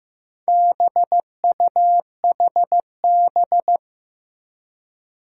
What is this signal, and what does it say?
Morse code, heard as a keyed tone, about 15 wpm, 700 Hz. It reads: BUHB
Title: Four Challenging Letter-Number Combinations 15wpm